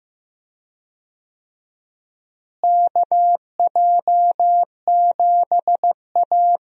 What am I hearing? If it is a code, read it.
Morse code, heard as a keyed tone, about 15 wpm, 700 Hz. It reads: KJ7A
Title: USA Call Signs - 4 Characters - Repeated 3x - 15wpm